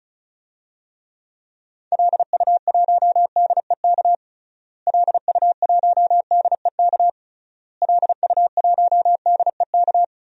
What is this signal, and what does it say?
Morse code, heard as a keyed tone, about 36 wpm, 700 Hz. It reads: LU1DEK LU1DEK LU1DEK
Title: DXCC-100 Call Signs - 6 Characters - Repeated 3x - 35wpm